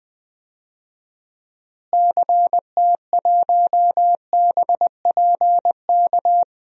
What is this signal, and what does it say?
Morse code, heard as a keyed tone, about 20 wpm, 700 Hz. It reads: CT1BPK
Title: DXCC-100 Call Signs - 6 Characters - Repeated 3x - 20wpm